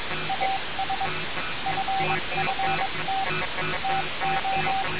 Morse code, heard as a keyed tone, about 22 wpm, 770 Hz, with noise. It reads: AS FFT TO
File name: radiocap / REC z